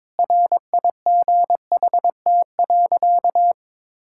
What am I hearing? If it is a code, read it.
Morse code, heard as a keyed tone, about 22 wpm, 700 Hz. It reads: RIGHT.
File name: Difficult Sentences with Hints - 22wpm